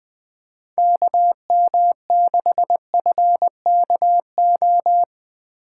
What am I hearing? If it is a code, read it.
Morse code, heard as a keyed tone, about 20 wpm, 700 Hz. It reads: KM6FKO